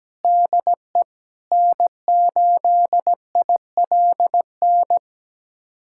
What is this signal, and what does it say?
Morse code, heard as a keyed tone, about 17 wpm, 700 Hz. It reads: DE N8ILN
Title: USA Callsign DE Callsign 17wpm